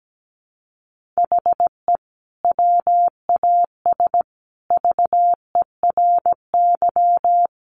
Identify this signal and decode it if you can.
Morse code, heard as a keyed tone, about 17 wpm, 700 Hz. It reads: HE WAS VERY
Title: Sentences from Top 1400 Words 17wpm